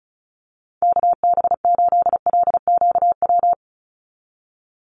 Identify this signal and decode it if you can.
Morse code, heard as a keyed tone, about 34 wpm, 700 Hz. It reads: KB8LQW
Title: USA Call Signs 35wpm